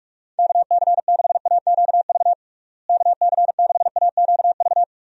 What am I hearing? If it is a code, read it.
Morse code, heard as a keyed tone, about 46 wpm, 700 Hz. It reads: KC6AQV KC6AQV
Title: USA Call Signs - 6 Characters - Repeated 2x - 45wpm